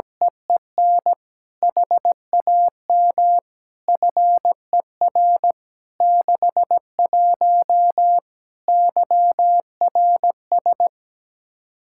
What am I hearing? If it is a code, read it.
Morse code, heard as a keyed tone, about 17 wpm, 700 Hz. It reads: EEN HAM FER 61 YRS
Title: K1USN SST QSO Practice 17wpm